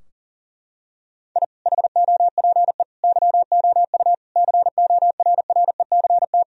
Morse code, heard as a keyed tone, about 40 wpm, 700 Hz. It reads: I HOPE YOU CORRECT
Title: Sentences from Top 1000 Words - Review Part 2 - 40wpm